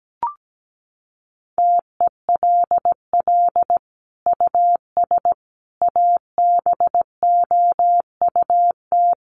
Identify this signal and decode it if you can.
Morse code, heard as a keyed tone, about 17 wpm, 700 Hz. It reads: TELL US ABOUT
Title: Sentences from Top 1400 Words 17wpm